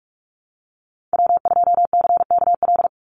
Morse code, heard as a keyed tone, about 46 wpm, 700 Hz. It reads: W2CKL
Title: NA SOTA Activator Callsigns 45wpm